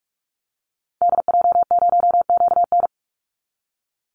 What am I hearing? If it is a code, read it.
Morse code, heard as a keyed tone, about 45 wpm, 700 Hz. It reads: DJ0QN
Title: Call Signs Repeated 2x - 45wpm